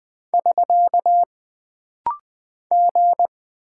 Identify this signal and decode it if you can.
Morse code, heard as a keyed tone, about 20 wpm, 700 Hz. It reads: <SK> G
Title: Single Character (Letter:Number:Punctuation:Prosign) - Rapid-Fire - 20wpm